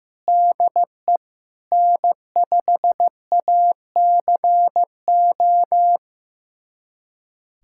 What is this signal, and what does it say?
Morse code, heard as a keyed tone, about 15 wpm, 700 Hz. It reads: DE N5ACO
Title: USA Callsign DE Callsign - 5 Characters - 15wpm